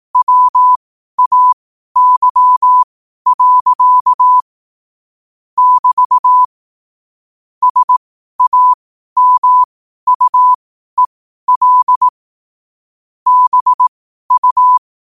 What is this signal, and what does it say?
Morse code, heard as a keyed tone, about 18 wpm, spaced as for 12 wpm, 1000 Hz. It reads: WAY. <BT> SAMUEL BU
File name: qotdfiles / 12wpm